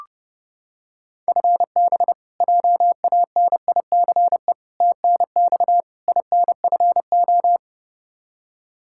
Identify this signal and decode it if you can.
Morse code, heard as a keyed tone, about 30 wpm, 700 Hz. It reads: FB JANICE TNX INFO
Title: K1USN SST QSO Practice 30wpm